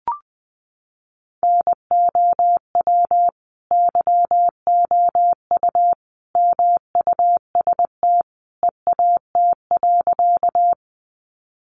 Morse code, heard as a keyed tone, about 20 wpm, 700 Hz. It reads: NOW YOU MUST EAT.
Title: Sentences from Top 300 Words 20wpm